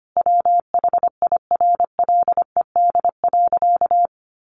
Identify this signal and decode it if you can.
Morse code, heard as a keyed tone, about 25 wpm, 700 Hz. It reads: WHIRLED.
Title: A Sound Of Thunder 25wpm